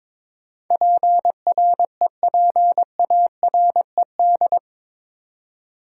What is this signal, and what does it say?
Morse code, heard as a keyed tone, about 22 wpm, 700 Hz. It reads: PREPARED
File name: practice / 22wpm / Eight Letter Words - 22wpm